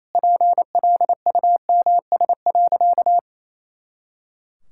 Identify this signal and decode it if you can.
Morse code, heard as a keyed tone, about 28 wpm, 700 Hz. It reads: PLUMS.
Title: Pangram Challenge v2 - 28wpm